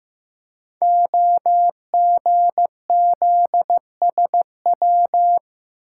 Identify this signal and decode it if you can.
Morse code, heard as a keyed tone, about 15 wpm, 700 Hz. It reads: OGZSW
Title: Code Groups - 5 Letter:Numbers - 15wpm